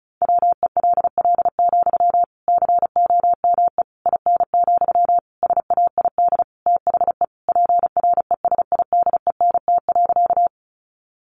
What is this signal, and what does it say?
Morse code, heard as a keyed tone, about 34 wpm, 700 Hz. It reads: WELL, COME IN, SAID THE PRESIDENT.
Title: Sentences from Top 1000 Words 35wpm